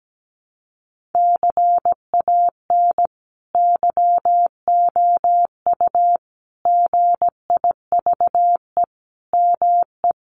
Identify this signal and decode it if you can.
Morse code, heard as a keyed tone, about 17 wpm, 700 Hz. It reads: CAN YOU GIVE ME
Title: Sentences from Top 1500 Words 17wpm